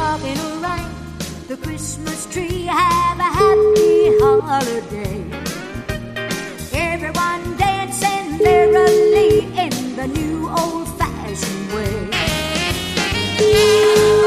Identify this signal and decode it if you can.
Ringback tone (Europe, 425 Hz), with noise.